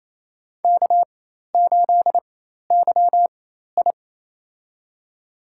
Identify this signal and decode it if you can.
Morse code, heard as a keyed tone, about 28 wpm, spaced as for 13 wpm, 700 Hz. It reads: K8YI
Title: USA Call Signs - 4x Character Spacing - 28wpm